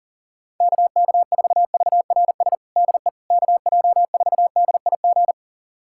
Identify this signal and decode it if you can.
Morse code, heard as a keyed tone, about 40 wpm, 700 Hz. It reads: KK4VRS DE KJ4DIG
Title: USA Callsign DE Callsign - 6 Characters - 40wpm